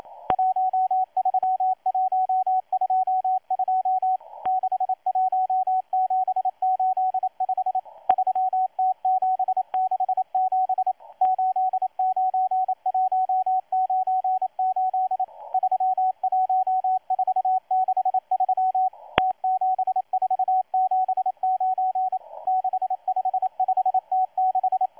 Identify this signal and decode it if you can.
Morse code, heard as a keyed tone, about 28 wpm, 750 Hz, with radio noise. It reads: 13122 61785 3T767 89198 31463 T7479 655T6
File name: M12-2016-07-07-1310utc-13926khz-error-restart